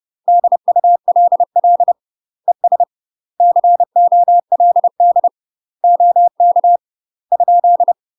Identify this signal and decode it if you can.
Morse code, heard as a keyed tone, about 30 wpm, 700 Hz. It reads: DULL ES COLD OK ?
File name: UK MASTER QSOs TRAINING / 30WPM / UK MASTER QSOs 0049